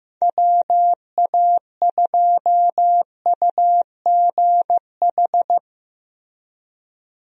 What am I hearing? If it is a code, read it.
Morse code, heard as a keyed tone, about 15 wpm, 700 Hz. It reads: WA2UGH
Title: USA Call Signs - 6 Characters - Repeated 2x - 15wpm